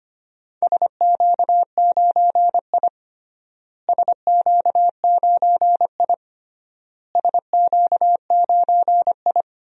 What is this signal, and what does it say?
Morse code, heard as a keyed tone, about 25 wpm, 700 Hz. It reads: SQ9I SQ9I SQ9I
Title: DXCC-100 Call Signs - 4 Characters - Repeated 3x - 25wpm